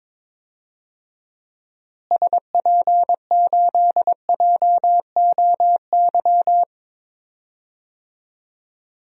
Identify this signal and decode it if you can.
Morse code, heard as a keyed tone, about 22 wpm, 700 Hz. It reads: SP8JOY
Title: DXCC-100 Call Signs - 6 Characters - Repeated 2x - 22wpm